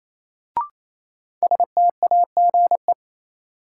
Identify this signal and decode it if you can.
Morse code, heard as a keyed tone, about 28 wpm, 700 Hz. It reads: STAGE
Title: Sentences from Top 1200 Words - New Word Drill - 28wpm